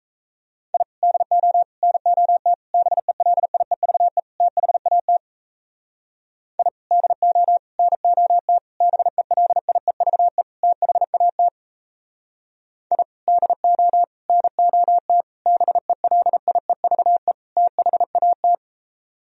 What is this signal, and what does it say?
Morse code, heard as a keyed tone, about 34 wpm, 700 Hz. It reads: I DO NOT BELIEVE THAT I DO NOT BELIEVE THAT I DO NOT BELIEVE THAT
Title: Speed-Racing - Sets of 5 Words 28wpm